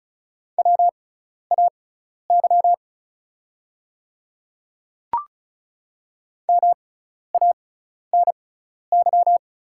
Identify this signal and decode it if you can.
Morse code, heard as a keyed tone, about 34 wpm, spaced as for 12 wpm, 700 Hz. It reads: WAY MANY
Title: 6x Character Spacing - Top 100 Words - 35wpm